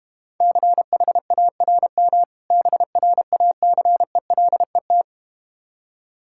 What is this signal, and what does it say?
Morse code, heard as a keyed tone, about 32 wpm, 690 Hz. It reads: CHARM BRACELET